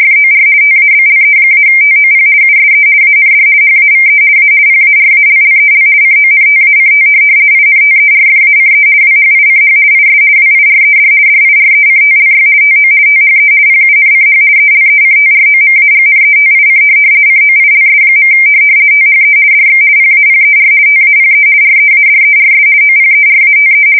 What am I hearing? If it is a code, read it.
RTTY, about 50 baud. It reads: FOR TABLE-TOP USE AND HAVE A PARALLEL CONTACT INTERFACE WHICH IS WIRED DIRECTLY TO THE PUNCH ELECTROMAGNETIC COILS. THE DX PAPER TAPE READERS OPERATE
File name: fsk170-50B / teletype-corp